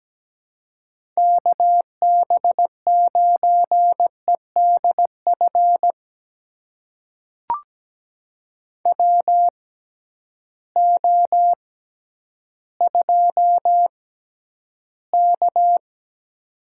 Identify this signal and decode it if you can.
Morse code, heard as a keyed tone, about 17 wpm, spaced as for 10 wpm, 700 Hz. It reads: KB9EDF WO2K
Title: USA Call Signs - 6x Character Spacing - 17wpm